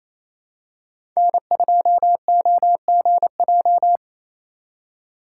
Morse code, heard as a keyed tone, about 28 wpm, 700 Hz. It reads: N2OGJ
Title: USA Call Signs - 5 Characters - 4x Character Spacing - 28wpm